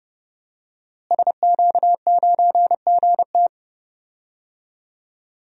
Morse code, heard as a keyed tone, about 30 wpm, 700 Hz. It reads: SQ9GT
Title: DXCC-100 Call Signs - 5 Characters - Repeated 3x - 30wpm